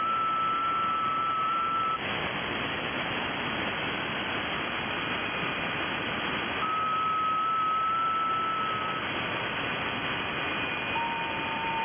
FT8.
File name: IC-7100